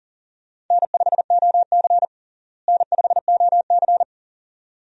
Morse code, heard as a keyed tone, about 40 wpm, 700 Hz. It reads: N5OC N5OC
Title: USA Call Signs - 4 Characters - Repeated 2x - 40wpm